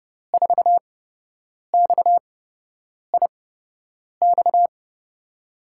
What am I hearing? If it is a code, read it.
Morse code, heard as a keyed tone, about 30 wpm, spaced as for 8 wpm, 700 Hz. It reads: 4XIX
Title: USA Call Signs - 5 Characters - 8x Character Spacing - 30wpm